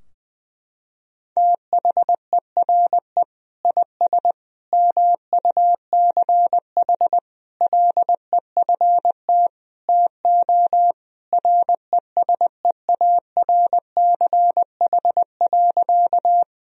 Morse code, heard as a keyed tone, about 20 wpm, 700 Hz. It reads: THERE IS MUCH LEFT TO RESEARCH.
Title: Sentences from Top 1000 Words - Review Part 4 - 20wpm